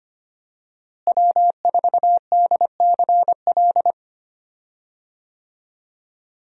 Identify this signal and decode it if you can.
Morse code, heard as a keyed tone, about 25 wpm, 700 Hz. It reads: W4DCL
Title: USA Call Signs - 5 Characters - Repeated 2x - 25wpm